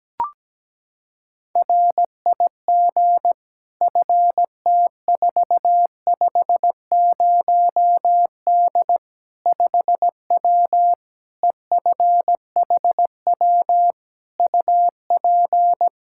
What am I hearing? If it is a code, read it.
Morse code, heard as a keyed tone, about 17 wpm, 700 Hz. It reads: RIG FT450D 5W EFHW UP
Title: Practice QSO - Exchange 2 - Info - 17wpm